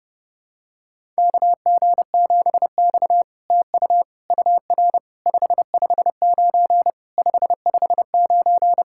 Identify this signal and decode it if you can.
Morse code, heard as a keyed tone, about 32 wpm, 700 Hz. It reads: KG7X TU UR 559 559
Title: SOTA QSO Practice 30wpm